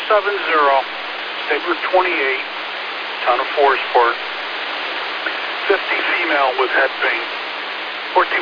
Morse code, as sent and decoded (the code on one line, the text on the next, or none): none